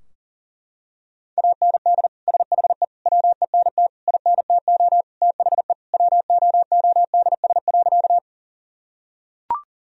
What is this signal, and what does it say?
Morse code, heard as a keyed tone, about 40 wpm, 700 Hz. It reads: AND SHE WENT INTO THE WOODS.
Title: Sentences from Top 1000 Words - Review Part 2 - 40wpm